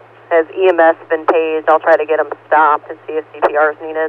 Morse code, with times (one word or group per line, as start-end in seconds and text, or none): none